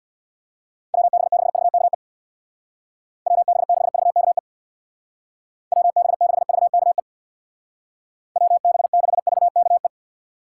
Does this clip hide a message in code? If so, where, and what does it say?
Yes: Morse code from 0.9 to 9.9 s, heard as a keyed tone, about 65 wpm, 700 Hz. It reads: WB6VKE WB6VKE WB6VKE WB6VKE